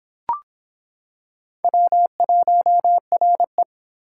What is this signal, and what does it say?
Morse code, heard as a keyed tone, about 26 wpm, 700 Hz. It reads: W1RE